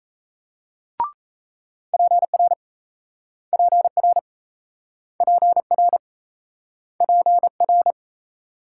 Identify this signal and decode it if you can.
Morse code, heard as a keyed tone, about 37 wpm, 700 Hz. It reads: PR PR PR PR